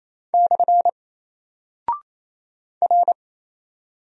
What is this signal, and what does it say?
Morse code, heard as a keyed tone, about 27 wpm, 700 Hz. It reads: / R
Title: Single Character (Letter:Number:Punctuation:Prosign) Alt 28wpm